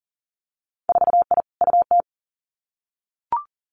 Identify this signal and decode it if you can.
Morse code, heard as a keyed tone, about 40 wpm, 700 Hz. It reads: 4I UT
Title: Winter Field Day Exchanges Encore 40wpm